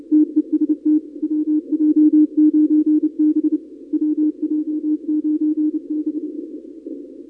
Morse code, heard as a keyed tone, about 29 wpm, 300 Hz. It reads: TEST WJ9B WJ9B
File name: contests / 2019-cw-wpx-kt5j-k5pi- -2330 0-31